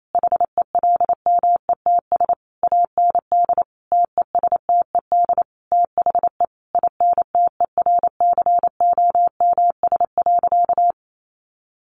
Morse code, heard as a keyed tone, about 28 wpm, 700 Hz. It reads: HELMETS AND TESTED THE INTERCOMS.
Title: A Sound Of Thunder 28wpm